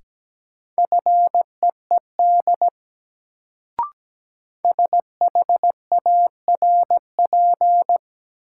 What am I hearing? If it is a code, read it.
Morse code, heard as a keyed tone, about 17 wpm, 700 Hz. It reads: FEED SHARP
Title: Sentences from Top 1600 Words - New Word Drill - 17wpm